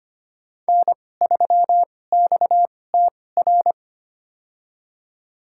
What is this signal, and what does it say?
Morse code, heard as a keyed tone, about 25 wpm, spaced as for 18 wpm, 700 Hz. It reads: N3XTR
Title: USA Call Signs - 5 Characters - 2x Character Spacing - 25wpm